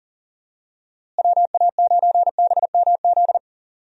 Morse code, heard as a keyed tone, about 40 wpm, 700 Hz. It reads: WA9BMZ
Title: USA Call Signs - 6 Characters - Repeated 2x - 40wpm